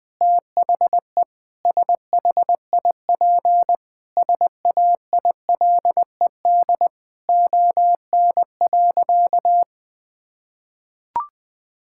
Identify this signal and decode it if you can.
Morse code, heard as a keyed tone, about 20 wpm, 700 Hz. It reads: THE SHIP SAILED ON.
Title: Sentences with 5-Word Phrases - Spoken Beginning - 20wpm